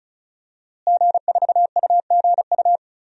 Morse code, heard as a keyed tone, about 36 wpm, 700 Hz. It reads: G4UGU
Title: DXCC-100 Call Signs - 5 Characters - Repeated 2x - 35wpm